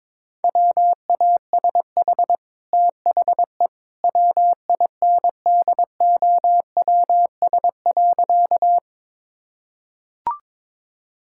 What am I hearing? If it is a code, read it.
Morse code, heard as a keyed tone, about 22 wpm, 690 Hz. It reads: WASH THE WINDOWS.
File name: practice / 22wpm / Easy Sentences with Hints - 22wpm